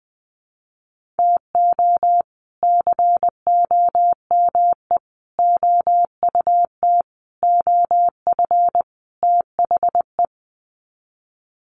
Morse code, heard as a keyed tone, about 20 wpm, 700 Hz. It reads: TO COME OUT OF THE
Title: Sets of 5 Words Encore2 20wpm